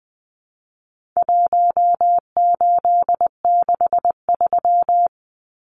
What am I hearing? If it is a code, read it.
Morse code, heard as a keyed tone, about 20 wpm, 700 Hz. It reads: 1863
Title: Numbers - Contest Style - 4-Digit 20wpm